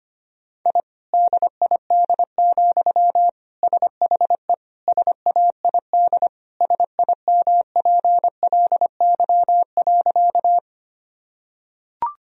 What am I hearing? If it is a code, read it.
Morse code, heard as a keyed tone, about 25 wpm, 700 Hz. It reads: I DID, SHE SAID SIMPLY.
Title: Sentences from Top 1000 Words - Review Part 4 - 25wpm